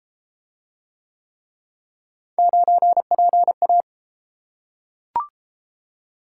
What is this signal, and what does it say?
Morse code, heard as a keyed tone, about 30 wpm, 700 Hz. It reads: 9PA